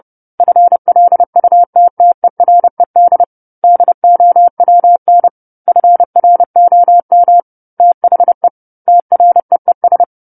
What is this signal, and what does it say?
Morse code, heard as a keyed tone, about 30 wpm, 700 Hz. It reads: FLUTTERED DOWN FROM THE TREES
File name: 10-Chapter-10 30 WPM